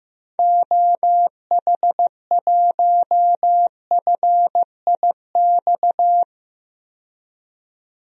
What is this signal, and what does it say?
Morse code, heard as a keyed tone, about 15 wpm, 700 Hz. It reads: OH1FIX
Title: DXCC-100 Call Signs - 6 Characters - 15wpm